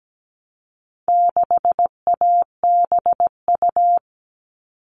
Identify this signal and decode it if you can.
Morse code, heard as a keyed tone, about 17 wpm, 700 Hz. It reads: 6ABU
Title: All Cut Numbers - 4-Digit 17wpm